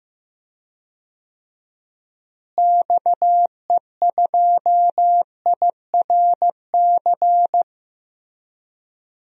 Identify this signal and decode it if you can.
Morse code, heard as a keyed tone, about 15 wpm, 700 Hz. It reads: XE2IRC
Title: DXCC-100 Call Signs - 6 Characters - Repeated 3x - 15wpm